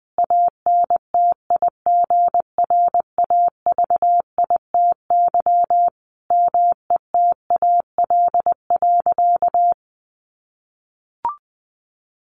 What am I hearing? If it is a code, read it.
Morse code, heard as a keyed tone, about 20 wpm, 700 Hz. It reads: ANTIGRAVITY METAL.
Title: A Sound Of Thunder 20wpm